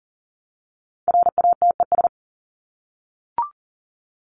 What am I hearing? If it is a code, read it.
Morse code, heard as a keyed tone, about 40 wpm, 700 Hz. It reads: RATES
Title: ES:S Suffix - Top 100 Words with Base Words 40wpm